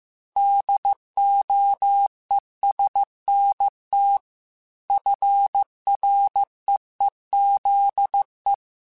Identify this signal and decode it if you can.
Morse code, heard as a keyed tone, about 15 wpm, 800 Hz. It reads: DOESNT FREEZE
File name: QSOs / QSOs-mp3s / WD0BGP 2@13